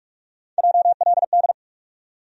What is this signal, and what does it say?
Morse code, heard as a keyed tone, about 47 wpm, 700 Hz. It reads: JLD
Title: USA Call Sign Postfixes - 3 Letters 45wpm